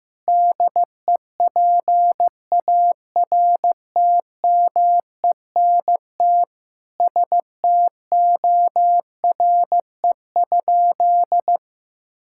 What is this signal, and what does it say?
Morse code, heard as a keyed tone, about 16 wpm, 700 Hz. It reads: DEPARTMENT STORE?
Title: Sentences from Top 1000 Words - Review Part 5 - 15wpm